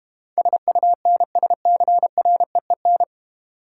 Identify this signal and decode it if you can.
Morse code, heard as a keyed tone, about 31 wpm, 700 Hz. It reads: SUNSCREEN
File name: Easy Words - Speed-Racing with Hints - 25wpm